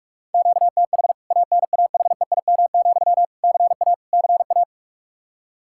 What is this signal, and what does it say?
Morse code, heard as a keyed tone, about 46 wpm, 700 Hz. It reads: QTH ANAHEIM, CA CA
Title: K1USN SST QSO Practice 45wpm